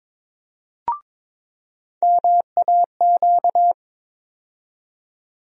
Morse code, heard as a keyed tone, about 22 wpm, 700 Hz. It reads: MAQ
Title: USA Call Sign Postfixes - 3 Letters 22wpm